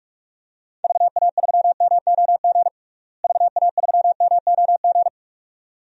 Morse code, heard as a keyed tone, about 43 wpm, 700 Hz. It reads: VA3MOG VA3MOG
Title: DXCC-100 Call Signs - 6 Characters - Repeated 2x - 45wpm